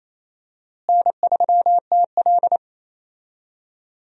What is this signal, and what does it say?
Morse code, heard as a keyed tone, about 28 wpm, 700 Hz. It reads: N3TL
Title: USA Call Signs - 4 Characters - Repeated 2x - 28wpm